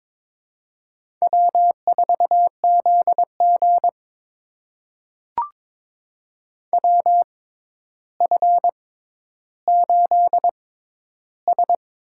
Morse code, heard as a keyed tone, about 22 wpm, spaced as for 11 wpm, 700 Hz. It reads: W4ZG WF8S
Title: USA Call Signs - 6x Character Spacing - 22wpm